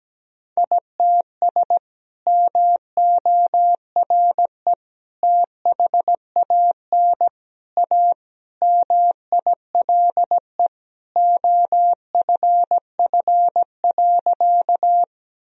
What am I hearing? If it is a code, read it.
Morse code, heard as a keyed tone, about 17 wpm, 700 Hz. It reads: ITS MORE THAN A MILE OFF.
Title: Sentences from Top 1000 Words - Review Part 1 - 17wpm